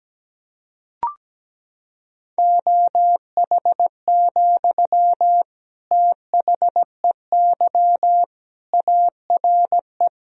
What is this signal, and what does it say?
Morse code, heard as a keyed tone, about 17 wpm, 700 Hz. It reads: OH, THEY ARE
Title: Sentences from Top 1000 Words - Review Part 4 - 17wpm